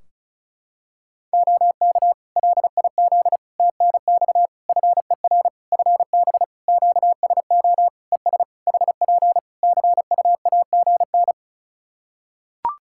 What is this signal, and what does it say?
Morse code, heard as a keyed tone, about 36 wpm, 700 Hz. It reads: OK LIZ TNX FER FB QSO ES HP CUAGN
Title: K1USN SST QSO Practice 35wpm